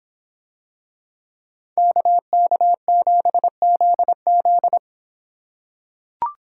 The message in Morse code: KK7ZZ

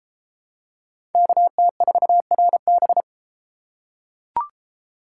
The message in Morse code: KT4RB

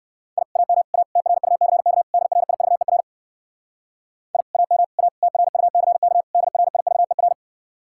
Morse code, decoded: I AM A TRUCK DRIVER I AM A TRUCK DRIVER

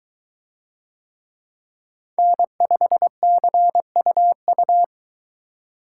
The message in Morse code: N5CUU

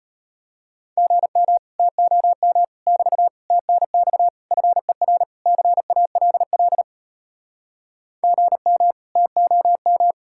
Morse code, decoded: GM TOM <BT> TNX FER CALL GM TOM

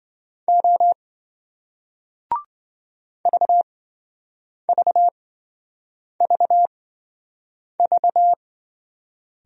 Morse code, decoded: O VVVV

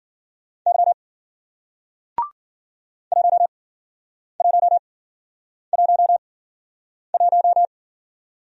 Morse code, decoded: Y 1111

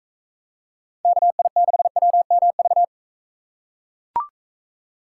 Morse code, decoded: KI6WMV

